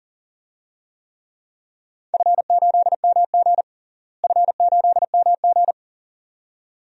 Morse code, heard as a keyed tone, about 37 wpm, 700 Hz. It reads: F8MG F8MG
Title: DXCC-100 Call Signs - 4 Characters - Repeated 2x - 40wpm